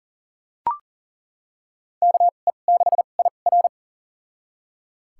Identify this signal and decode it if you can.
Morse code, heard as a keyed tone, about 40 wpm, spaced as for 29 wpm, 700 Hz. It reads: KE6IR
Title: USA Call Signs - 5 Characters - 2x Character Spacing - 40wpm